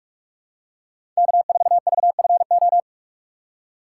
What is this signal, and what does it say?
Morse code, heard as a keyed tone, about 41 wpm, 700 Hz. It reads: K4VFO
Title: USA Call Signs - 3x Character Spacing - 45wpm